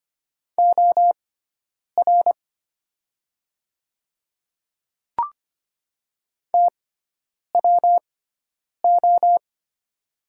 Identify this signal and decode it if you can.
Morse code, heard as a keyed tone, about 25 wpm, spaced as for 9 wpm, 700 Hz. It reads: OR TWO
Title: 6x Character Spacing - Top 100 Words - 25wpm